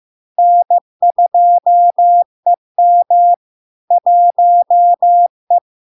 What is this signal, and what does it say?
Morse code, heard as a keyed tone, about 15 wpm, 700 Hz. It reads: N2EM 1E
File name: LARK-FD-2022-Log-in-CW-15wpm-700-Hz